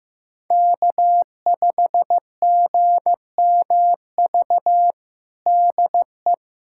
Morse code, heard as a keyed tone, about 15 wpm, 700 Hz. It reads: K5GMV DE
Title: USA Callsign DE Callsign - 5 Characters - 15wpm